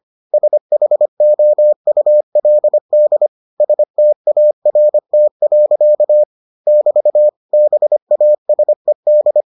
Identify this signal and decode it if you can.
Morse code, heard as a keyed tone, about 25 wpm, 600 Hz. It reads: SHOULD START. <BT> BASED